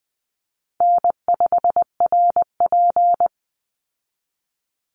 Morse code, heard as a keyed tone, about 20 wpm, 700 Hz. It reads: N5RP
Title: Easy Call Signs Repeated 3x - 20wpm